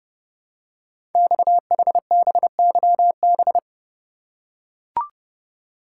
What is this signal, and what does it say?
Morse code, heard as a keyed tone, about 30 wpm, 700 Hz. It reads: XHBYB
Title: Code Groups - 5 Letters - 30wpm